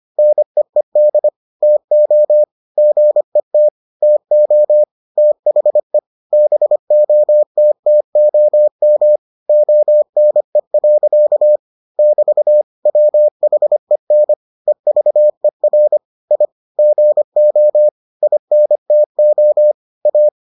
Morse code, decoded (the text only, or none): NEED TO GET TO THE BOTTOM ONE. <BT> WHEN EVER I GO INTO A